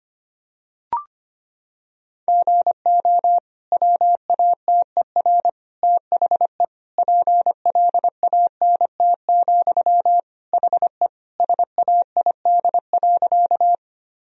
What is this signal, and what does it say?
Morse code, heard as a keyed tone, about 25 wpm, 700 Hz. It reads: GO WATER THE PLANT, HE SAID.
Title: Sentences from Top 1000 Words - Review Part 1 - 25wpm